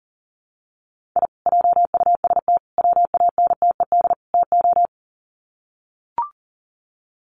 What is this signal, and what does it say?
Morse code, heard as a keyed tone, about 37 wpm, 700 Hz. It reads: I JUST WANTED TO
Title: Sets of 4 Words 40wpm